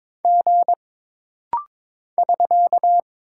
Morse code, heard as a keyed tone, about 22 wpm, 700 Hz. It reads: G <SK>
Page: This rapid-fire timing is geared to push your ICR (Instant Character Recognition) skills.